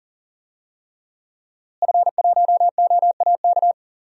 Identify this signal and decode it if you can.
Morse code, heard as a keyed tone, about 39 wpm, 700 Hz. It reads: F1OAK
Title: DXCC-100 Call Signs - 5 Characters - Repeated 3x - 40wpm